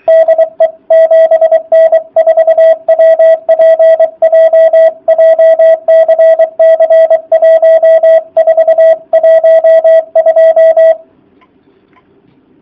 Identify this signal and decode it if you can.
Morse code, heard as a keyed tone, about 23 wpm, 650 Hz, with radio noise. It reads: DE 7N4WPJ JCC1412